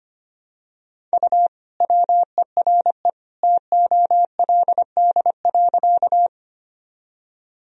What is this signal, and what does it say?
Morse code, heard as a keyed tone, about 25 wpm, 700 Hz. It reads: U WERE TOLD.
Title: Sentences from Top 400 Words and Abbreviations 25wpm